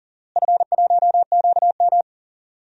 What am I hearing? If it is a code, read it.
Morse code, heard as a keyed tone, about 39 wpm, 700 Hz. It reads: F1QM
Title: DXCC-100 Call Signs - 4 Characters - Repeated 3x - 40wpm